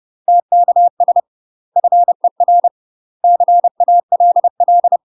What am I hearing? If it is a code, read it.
Morse code, heard as a keyed tone, about 30 wpm, 700 Hz. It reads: TKS FER CALL
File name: UK MASTER QSOs TRAINING / 30WPM / UK MASTER QSOs 0046